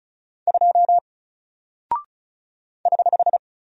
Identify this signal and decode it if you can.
Morse code, heard as a keyed tone, about 36 wpm, 700 Hz. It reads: 2 <HH>
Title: Single Character (Letter:Number:Punctuation:Prosign) 35wpm